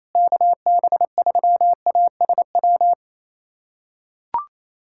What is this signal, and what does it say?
Morse code, heard as a keyed tone, about 28 wpm, 700 Hz. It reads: KB3ASW
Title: USA Callsign DE Callsign - 6 Characters - 28wpm